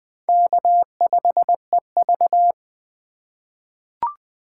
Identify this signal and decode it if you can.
Morse code, heard as a keyed tone, about 20 wpm, 700 Hz. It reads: K5EV
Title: USA Call Signs - 4 Characters - 2x Character Spacing - 20wpm